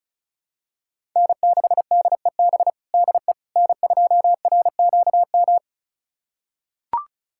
Morse code, heard as a keyed tone, about 34 wpm, 700 Hz. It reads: N6DEB DE N2RQM
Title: USA Callsign DE Callsign - 5 Characters - 35wpm